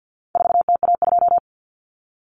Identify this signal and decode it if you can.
Morse code, heard as a keyed tone, about 50 wpm, 700 Hz. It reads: 4TA2
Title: Common Cut Numbers - 4-Digit 50wpm